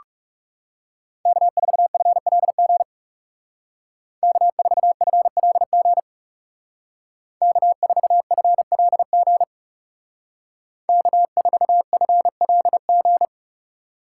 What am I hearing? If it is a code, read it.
Morse code, heard as a keyed tone, about 36 wpm, 700 Hz. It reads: K4FLG K4FLG K4FLG K4FLG
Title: Speed Racing - USA 1x3 Call Signs 30wpm